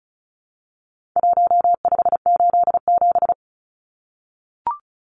Morse code, heard as a keyed tone, about 34 wpm, 700 Hz. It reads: 1587
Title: Numbers - 4-Digit 35wpm